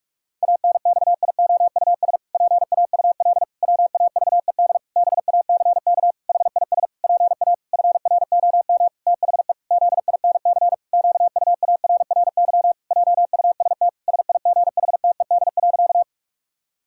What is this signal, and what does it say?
Morse code, heard as a keyed tone, about 46 wpm, 700 Hz. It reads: ANXIOUS PAUL WAVED BACK HIS PA FROM THE ZINC QUARRY JUST SIGHTED.
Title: Pangram Challenge v2 - 45wpm